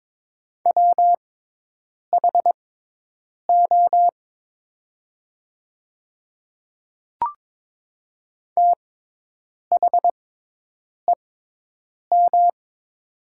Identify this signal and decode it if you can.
Morse code, heard as a keyed tone, about 22 wpm, spaced as for 8 wpm, 700 Hz. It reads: WHO THEM
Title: 6x Character Spacing - Top 100 Words - 22wpm